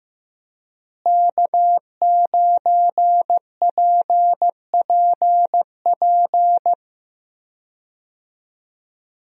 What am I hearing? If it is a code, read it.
Morse code, heard as a keyed tone, about 15 wpm, 700 Hz. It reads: K9PPP
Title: USA Call Signs - 5 Characters - Repeated 3x - 15wpm